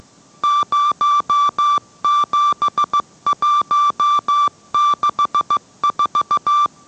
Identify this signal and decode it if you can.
Morse code, heard as a keyed tone, about 15 wpm, 1170 Hz. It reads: 07164